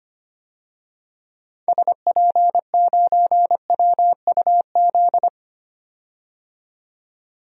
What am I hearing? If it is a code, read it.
Morse code, heard as a keyed tone, about 25 wpm, 700 Hz. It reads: SP9WUZ
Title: DXCC-100 Call Signs - 6 Characters - 25wpm